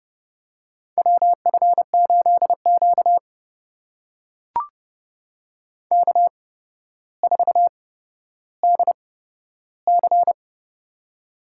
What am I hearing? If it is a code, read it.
Morse code, heard as a keyed tone, about 30 wpm, spaced as for 13 wpm, 700 Hz. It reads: WF8Q K4DC
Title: USA Call Signs - 4 Characters - 8x Character Spacing - 30wpm